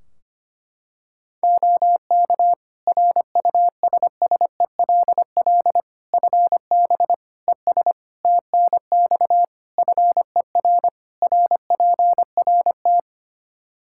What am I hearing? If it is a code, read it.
Morse code, heard as a keyed tone, about 25 wpm, 700 Hz. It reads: OK RUSSELL FB ES TNX FER RPRT
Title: Practice QSO - Exchange 2 - Info - 25wpm